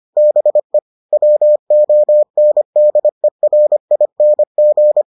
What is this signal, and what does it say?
Morse code, heard as a keyed tone, about 25 wpm, 600 Hz. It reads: BE WONDERING